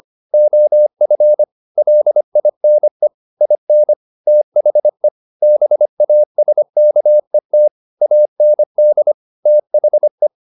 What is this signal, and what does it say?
Morse code, heard as a keyed tone, about 25 wpm, 600 Hz. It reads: OF LINE IN THE BASKET AND THE
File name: the old man and the sea part 1 25wpm